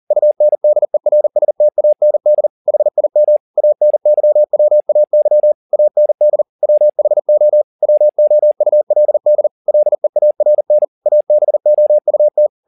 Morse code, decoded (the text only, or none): UNDERSTAND HIM ANYWAY AND WHO WOULD LEARN ABOUT